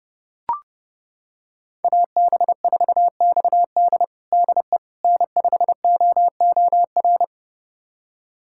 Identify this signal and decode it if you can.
Morse code, heard as a keyed tone, about 30 wpm, 700 Hz. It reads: AB4XD DE N5OOR
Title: USA Callsign DE Callsign - 5 Characters - 30wpm